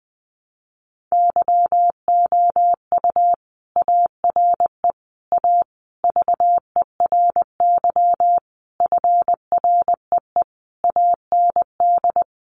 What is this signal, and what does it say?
Morse code, heard as a keyed tone, about 20 wpm, 700 Hz. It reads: YOU ARE A VERY FREE AND
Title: Sentences from Top 1000 Words 20wpm